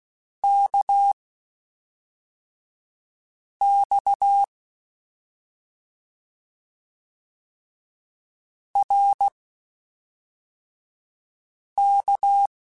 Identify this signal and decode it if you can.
Morse code, heard as a keyed tone, about 16 wpm, spaced as for 3 wpm, 800 Hz. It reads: KX RK